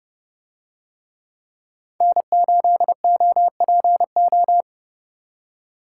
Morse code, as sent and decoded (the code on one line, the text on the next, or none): -. ---.. --- .--. ---
N8OPO